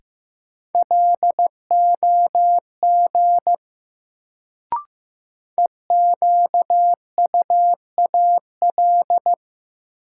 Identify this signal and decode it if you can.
Morse code, heard as a keyed tone, about 15 wpm, 700 Hz. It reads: LOG EQUAL
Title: Sentences from Top 1400 Words - New Word Drill - 15wpm